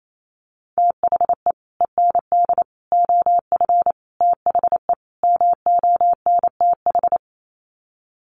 Morse code, heard as a keyed tone, about 28 wpm, 700 Hz. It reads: THE END OF THE MONTH
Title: Sets of 5 Words Encore 28wpm